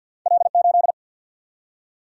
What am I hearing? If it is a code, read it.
Morse code, heard as a keyed tone, about 50 wpm, 700 Hz. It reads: L 8
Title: Single-Letter Single-Number 50wpm